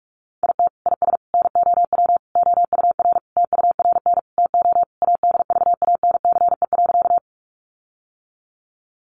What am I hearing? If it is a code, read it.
Morse code, heard as a keyed tone, about 44 wpm, 700 Hz. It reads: IT IS NOW OUR TURN TO ADVANCE.
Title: Sentences from Top 1400 Words 45wpm